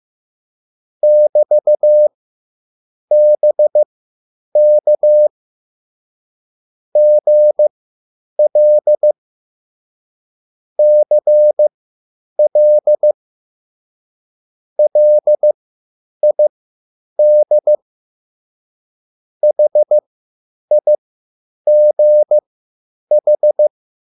Morse code, heard as a keyed tone, about 15 wpm, spaced as for 8 wpm, 600 Hz. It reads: <BT>BK GL CL LID HIGH
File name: QSL / CW lesson